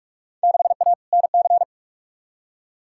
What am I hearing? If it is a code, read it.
Morse code, heard as a keyed tone, about 47 wpm, 700 Hz. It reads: 6A NC
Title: Field Day Exchanges Encore - 45wpm